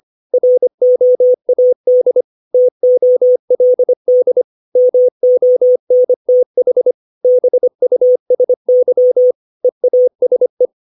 Morse code, heard as a keyed tone, about 25 wpm, 500 Hz. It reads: ROAD TOLD MONTH BUSY EASE
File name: 632 words / 632 Words 25WPM